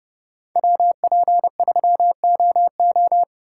Morse code, heard as a keyed tone, about 35 wpm, 700 Hz. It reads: WP3OO
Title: Speed Racing - USA 2x2 Call Signs 30wpm